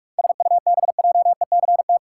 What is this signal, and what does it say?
Morse code, heard as a keyed tone, about 44 wpm, 700 Hz. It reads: SUBJECT
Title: Sentences from Top 800 Words - New Word Drill - 45wpm